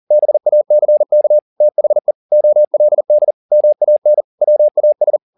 Morse code, heard as a keyed tone, about 40 wpm, 600 Hz. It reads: BACK THE OLD MAN WAS